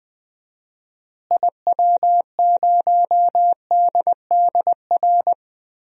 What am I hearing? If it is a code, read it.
Morse code, heard as a keyed tone, about 20 wpm, 700 Hz. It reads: IW0DDR
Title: DXCC-100 Call Signs - 6 Characters - Repeated 3x - 20wpm